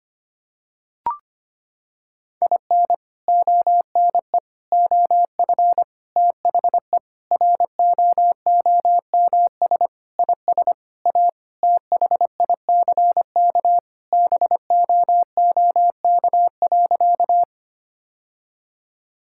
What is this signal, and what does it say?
Morse code, heard as a keyed tone, about 25 wpm, 700 Hz. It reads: IN ONE OF THE ROOMS IS A THICK BOOK.
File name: practice / 25wpm / Sentences with 4-Word Phrases - 25wpm